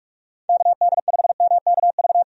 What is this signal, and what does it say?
Morse code, heard as a keyed tone, about 44 wpm, 700 Hz. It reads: KD5MKV
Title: USA Call Signs - 6 Characters - 45wpm